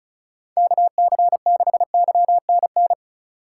Morse code, heard as a keyed tone, about 36 wpm, 700 Hz. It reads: KC6YNN
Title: USA Call Signs - 4x Character Spacing - 35wpm